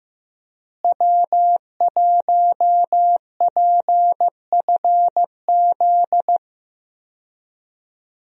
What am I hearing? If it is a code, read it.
Morse code, heard as a keyed tone, about 15 wpm, 700 Hz. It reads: W1PFZ
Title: USA Call Signs - 5 Characters - Repeated 3x - 15wpm